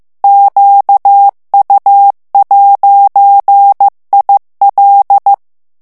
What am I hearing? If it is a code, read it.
Morse code, heard as a keyed tone, about 15 wpm, 800 Hz. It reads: QU'IL
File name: morse mp3 / FRANCAIS TEXTE CLAIR 2 VITESSE 14 MOTS MINUTES